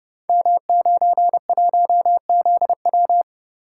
Morse code, heard as a keyed tone, about 30 wpm, 700 Hz. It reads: M91ZW
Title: Code Groups - 5 Letter:Numbers - 30wpm